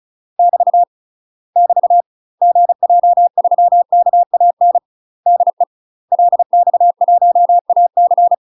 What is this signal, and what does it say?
Morse code, heard as a keyed tone, about 34 wpm, 700 Hz. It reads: <BT> <BT> GJ3KAN DE LX1AC
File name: UK MASTER QSOs TRAINING / 35WPM / UK MASTER QSOs 0021